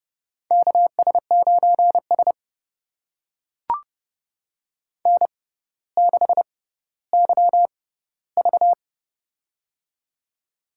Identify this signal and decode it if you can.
Morse code, heard as a keyed tone, about 32 wpm, spaced as for 16 wpm, 700 Hz. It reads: KS9S N6YV
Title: USA Call Signs - 4 Characters - 6x Character Spacing - 30wpm